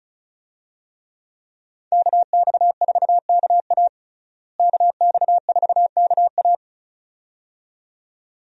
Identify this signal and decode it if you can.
Morse code, heard as a keyed tone, about 34 wpm, 700 Hz. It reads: KX4KA KX4KA
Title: USA Call Signs - 5 Characters - Repeated 2x - 35wpm